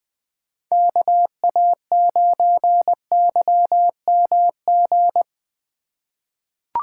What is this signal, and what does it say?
Morse code, heard as a keyed tone, about 20 wpm, 700 Hz. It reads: KA9YMG
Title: K1USN SST QSO Practice - 20wpm